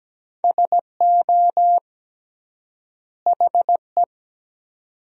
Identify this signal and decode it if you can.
Morse code, heard as a keyed tone, about 17 wpm, 700 Hz. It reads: SO HE
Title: 3x Word-Spacing - Sentences from Top 300 Words - 17wpm